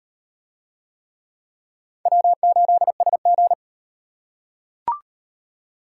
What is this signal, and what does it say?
Morse code, heard as a keyed tone, about 38 wpm, 700 Hz. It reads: W8SG